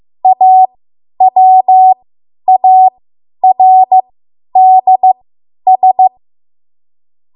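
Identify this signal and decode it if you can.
Morse code, heard as a keyed tone, about 15 wpm, spaced as for 10 wpm, 750 Hz. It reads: AWARDS
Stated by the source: Morse code practice run at 10 WPM from a code practice archive.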